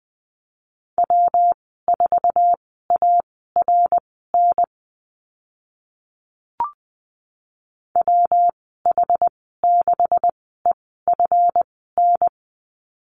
Morse code, heard as a keyed tone, about 20 wpm, spaced as for 14 wpm, 700 Hz. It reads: W4ARN WH6EFN
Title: 147 Lesson 24g - Letter y - Call Signs 20wpm